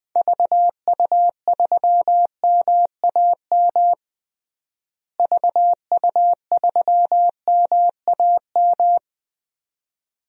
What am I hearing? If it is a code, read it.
Morse code, heard as a keyed tone, about 20 wpm, 700 Hz. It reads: VU3MAM VU3MAM
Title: DXCC-100 Call Signs - 6 Characters - Repeated 2x - 20wpm